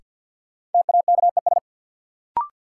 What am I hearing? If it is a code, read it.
Morse code, heard as a keyed tone, about 53 wpm, 700 Hz. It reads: TAKES
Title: Sentences from Top 1000 Words - Review Part 6 - New Word Drill - 50wpm